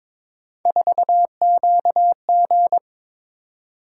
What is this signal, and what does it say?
Morse code, heard as a keyed tone, about 22 wpm, 700 Hz. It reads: 4QG
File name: practice / 22wpm / Three Letter-Number Combinations - 22wpm